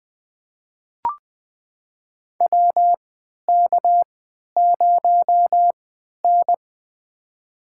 Morse code, heard as a keyed tone, about 20 wpm, spaced as for 11 wpm, 700 Hz. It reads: WK0N
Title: USA Call Signs - 4 Characters - 3x Character Spacing - 20wpm